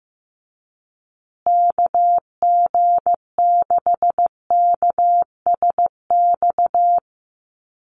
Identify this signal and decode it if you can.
Morse code, heard as a keyed tone, about 15 wpm, 700 Hz. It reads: KG6KSX
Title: USA Call Signs Repeated 3x 15wpm